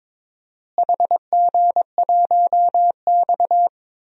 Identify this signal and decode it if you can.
Morse code, heard as a keyed tone, about 22 wpm, 700 Hz. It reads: HG1X
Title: DXCC-100 Call Signs - 4 Characters - Repeated 2x - 22wpm